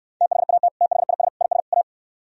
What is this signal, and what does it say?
Morse code, heard as a keyed tone, about 69 wpm, 700 Hz. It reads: THAT THIS IS A